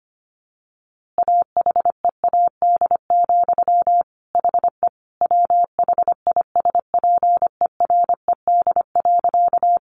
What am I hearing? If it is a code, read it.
Morse code, heard as a keyed tone, about 25 wpm, 700 Hz. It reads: AHEAD, HE WHISPERED.
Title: A Sound Of Thunder 25wpm